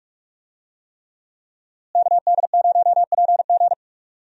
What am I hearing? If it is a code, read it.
Morse code, heard as a keyed tone, about 41 wpm, 700 Hz. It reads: KD0PG
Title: USA Call Signs - 5 Characters - Repeated 3x - 45wpm